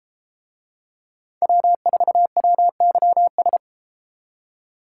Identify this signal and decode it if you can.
Morse code, heard as a keyed tone, about 33 wpm, 700 Hz. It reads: W4WYS